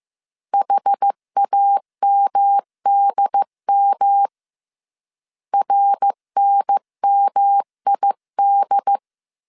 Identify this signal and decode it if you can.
Morse code, heard as a keyed tone, about 15 wpm, 800 Hz. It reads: HAMDM RNMID
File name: CW / DET / 1 3 1500C S C 14wpm 2458